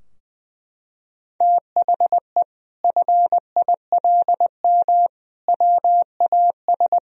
Morse code, heard as a keyed tone, about 20 wpm, 700 Hz. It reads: THE FILM WAS
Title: Sentences from Top 1000 Words - Review Part 3 - 20wpm